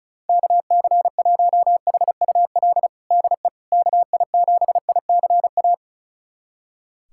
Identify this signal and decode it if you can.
Morse code, heard as a keyed tone, about 36 wpm, 700 Hz. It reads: KC1HUL DE KI7ICA
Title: USA Callsign DE Callsign - 6 Characters - 35wpm